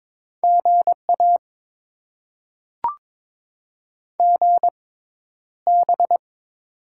Morse code, heard as a keyed tone, about 22 wpm, spaced as for 11 wpm, 700 Hz. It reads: GA GB
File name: Top 90 QSO Elements - 6x Character Spacing - 22wpm